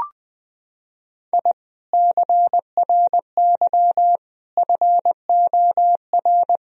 Morse code, heard as a keyed tone, about 20 wpm, 700 Hz. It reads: I CRY FOR
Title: Sentences from Top 1000 Words - Review Part 2 - 20wpm